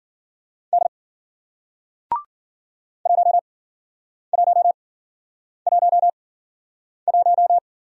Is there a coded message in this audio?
Yes: Morse code from 0.7 to 7.6 s, heard as a keyed tone, about 53 wpm, spaced as for 9 wpm, 700 Hz. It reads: D 1111